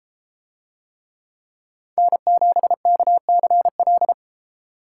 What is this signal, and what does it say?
Morse code, heard as a keyed tone, about 33 wpm, 700 Hz. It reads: N7KCL